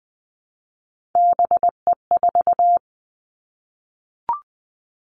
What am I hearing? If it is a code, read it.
Morse code, heard as a keyed tone, about 20 wpm, 700 Hz. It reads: BE4
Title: Three Challenging Letter-Number Combinations 20wpm